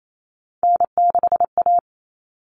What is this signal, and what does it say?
Morse code, heard as a keyed tone, about 28 wpm, 700 Hz. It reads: N6A
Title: All Cut Numbers - 3-Digit 28wpm